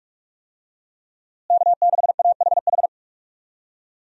Morse code, heard as a keyed tone, about 41 wpm, 700 Hz. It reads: K6AHH
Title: USA Call Signs - 5 Characters - Repeated 3x - 45wpm